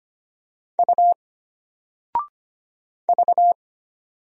Morse code, heard as a keyed tone, about 25 wpm, 700 Hz. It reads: U V
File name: Single-Letter-Number-25wpm